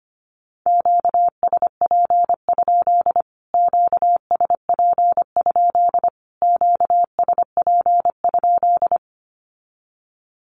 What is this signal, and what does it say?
Morse code, heard as a keyed tone, about 25 wpm, 700 Hz. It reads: QSP? QSP? QSP?
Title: Q-Codes Repeated 3x 25wpm